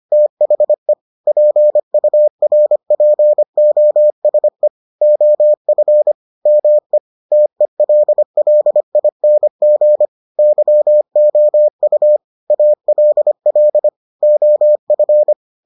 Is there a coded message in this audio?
Yes: Morse code from 0.1 to 15.3 s, heard as a keyed tone, about 25 wpm, 600 Hz. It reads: THE PURPOSE OF ME TELLING YOU ALL OF